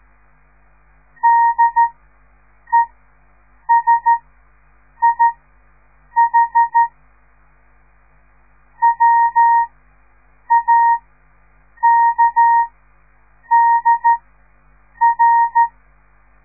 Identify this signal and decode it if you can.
Morse code, heard as a keyed tone, about 13 wpm, spaced as for 7 wpm, 930 Hz. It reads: DESIH WAKDR
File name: radiomorningstar / morse / les 6